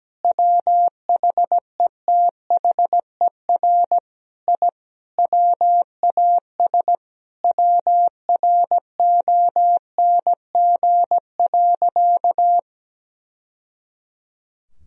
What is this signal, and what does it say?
Morse code, heard as a keyed tone, about 17 wpm, 700 Hz. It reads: WHETHER I WAS WRONG.
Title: Sentences from Top 1000 Words - Review Part 3 - 17wpm